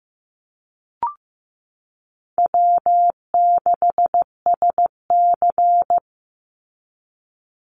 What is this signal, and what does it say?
Morse code, heard as a keyed tone, about 15 wpm, 700 Hz. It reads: W6SC
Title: Easy Call Signs Repeated 2x - 15wpm